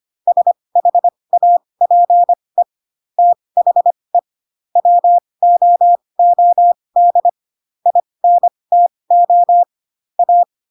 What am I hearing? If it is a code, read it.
Morse code, heard as a keyed tone, about 25 wpm, spaced as for 20 wpm, 700 Hz. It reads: SHAPE THE WOOD INTO A